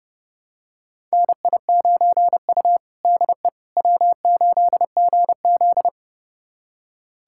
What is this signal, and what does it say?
Morse code, heard as a keyed tone, about 30 wpm, 700 Hz. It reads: NI9U DE W8GZ
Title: USA Callsign DE Callsign - 4 Characters - 30wpm